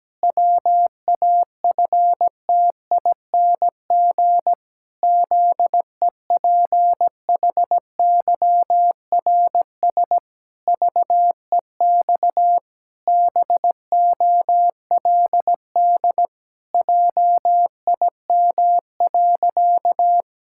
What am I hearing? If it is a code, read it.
Morse code, heard as a keyed tone, about 17 wpm, 700 Hz. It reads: WAFTING ZEPHYRS VEX BOLD JIM.
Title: Pangram Challenge v2 - 17wpm